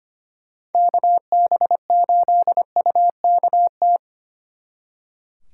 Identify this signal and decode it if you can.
Morse code, heard as a keyed tone, about 27 wpm, 700 Hz. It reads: KB8UKT
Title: Speed Racing - USA 2x3 Call Signs 25wpm